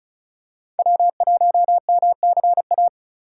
Morse code, heard as a keyed tone, about 36 wpm, 690 Hz. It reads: W1MCA
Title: USA Call Signs - 5 Characters - 3x Character Spacing - 35wpm